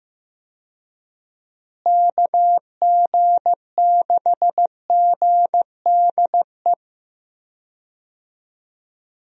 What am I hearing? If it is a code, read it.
Morse code, heard as a keyed tone, about 15 wpm, 700 Hz. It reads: KG6GDE
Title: USA Call Signs - 6 Characters - Repeated 2x - 15wpm